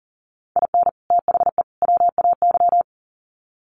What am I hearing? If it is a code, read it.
Morse code, heard as a keyed tone, about 40 wpm, 700 Hz. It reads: IN THE WAY
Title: Sets of 3 Words 40wpm